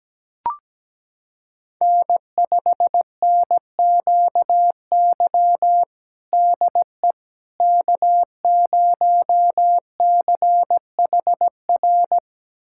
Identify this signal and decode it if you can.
Morse code, heard as a keyed tone, about 17 wpm, 700 Hz. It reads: N5NQY DE K0CHR
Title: USA Callsign DE Callsign - 5 Characters - 17wpm